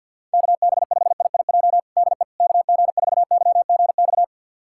Morse code, heard as a keyed tone, about 50 wpm, 700 Hz. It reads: KB5IIJ DE KG4YGX
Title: USA Callsign DE Callsign - 6 Characters - 50wpm